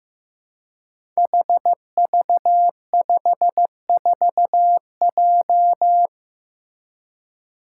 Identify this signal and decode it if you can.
Morse code, heard as a keyed tone, about 15 wpm, 700 Hz. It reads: HV54J
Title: Five Challenging Letter-Number Combinations 15wpm